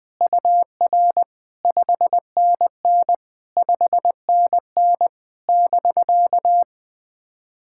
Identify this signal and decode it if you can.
Morse code, heard as a keyed tone, about 20 wpm, 690 Hz. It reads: UR 5NN 5NN <BK>
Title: SOTA QSO Practice 20wpm